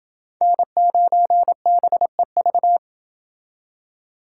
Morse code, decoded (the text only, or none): N9BEV